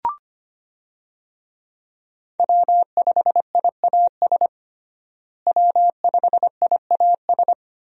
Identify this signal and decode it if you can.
Morse code, heard as a keyed tone, about 25 wpm, 700 Hz. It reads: W5IAS W5IAS
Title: USA Call Signs - 5 Characters - Repeated 3x - 25wpm